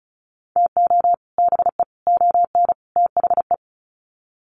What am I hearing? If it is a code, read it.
Morse code, heard as a keyed tone, about 34 wpm, 700 Hz. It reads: TO BE ON THE
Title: Sets of 4 Words 35wpm